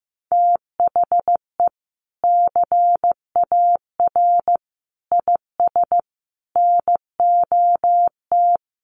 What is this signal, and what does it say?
Morse code, heard as a keyed tone, about 15 wpm, 700 Hz. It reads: THE CAR IS NOT
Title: Sentences from Top 1400 Words 15wpm